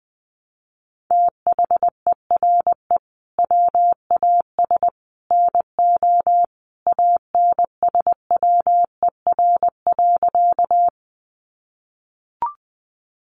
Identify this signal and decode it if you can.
Morse code, heard as a keyed tone, about 20 wpm, 700 Hz. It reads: THERE WAS NO ANSWER.
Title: Sentences from Top 300 Words 20wpm